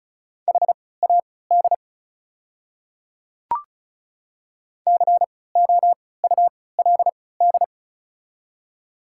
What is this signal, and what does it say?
Morse code, heard as a keyed tone, about 34 wpm, spaced as for 20 wpm, 690 Hz. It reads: HAD COULD
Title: 3x Character Spacing - Top 500 Words - 35wpm